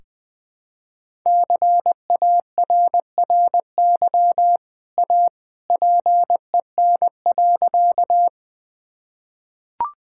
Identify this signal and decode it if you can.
Morse code, heard as a keyed tone, about 20 wpm, 700 Hz. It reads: CARRY A PEN.
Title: Easy Sentences with Hints - 20wpm